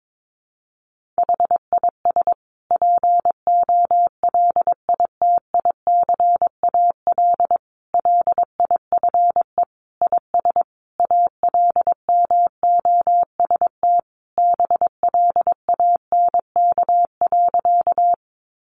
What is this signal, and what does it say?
Morse code, heard as a keyed tone, about 22 wpm, 700 Hz. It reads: HIS POLITICAL LIFE IS ALMOST BLANK.
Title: Sentences from Top 1500 Words 22wpm